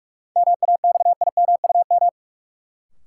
Morse code, heard as a keyed tone, about 46 wpm, 700 Hz. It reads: MAXIMUM
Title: Sentences from Top 1300 Words - New Word Drill - 45wpm